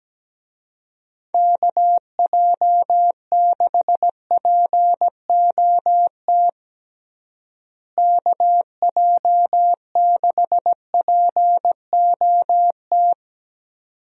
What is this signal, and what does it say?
Morse code, heard as a keyed tone, about 17 wpm, 700 Hz. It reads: KJ6POT KJ6POT
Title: USA Call Signs - 6 Characters - Repeated 2x - 17wpm